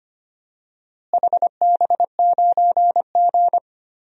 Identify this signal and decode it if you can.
Morse code, heard as a keyed tone, about 25 wpm, 700 Hz. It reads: HB9G
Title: DXCC-100 Call Signs - 4 Characters - Repeated 3x - 25wpm